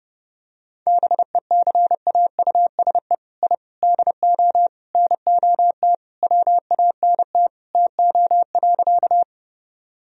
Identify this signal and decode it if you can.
Morse code, heard as a keyed tone, about 30 wpm, 700 Hz. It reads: BECAUSE I DO NOT WANT TO.
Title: Sentences from Top 1000 Words - Review Part 3 - 30wpm